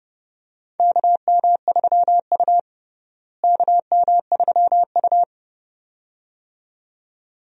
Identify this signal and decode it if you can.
Morse code, heard as a keyed tone, about 32 wpm, 700 Hz. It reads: KM3U KM3U
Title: USA Call Signs - 4 Characters - Repeated 2x - 30wpm